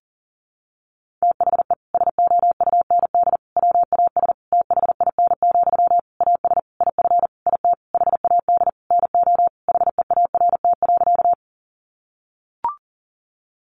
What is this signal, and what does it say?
Morse code, heard as a keyed tone, about 40 wpm, 700 Hz. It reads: THE SOUND WAS THIN, AS IF IT HAD NO HEART.
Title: Sentences from Top 1200 Words 40wpm